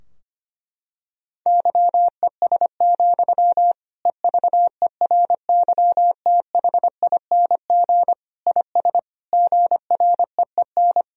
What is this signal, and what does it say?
Morse code, heard as a keyed tone, about 25 wpm, 700 Hz. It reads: YES, EVERYTHING IS GREEN
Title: Sentences from Top 1000 Words - Review Part 2 - 25wpm